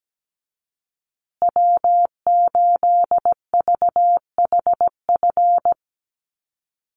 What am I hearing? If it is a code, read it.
Morse code, heard as a keyed tone, about 17 wpm, 700 Hz. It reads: W8VHF
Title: USA Call Signs Repeated 2x 17wpm